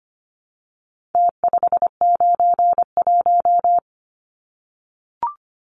Morse code, heard as a keyed tone, about 25 wpm, 700 Hz. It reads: T591
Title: Numbers - Contest Style - 4-Digit 25wpm